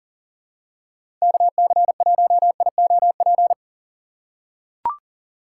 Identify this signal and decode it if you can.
Morse code, heard as a keyed tone, about 40 wpm, 700 Hz. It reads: KC1IOP
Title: USA Call Signs - 6 Characters - 6x Character Spacing - 40wpm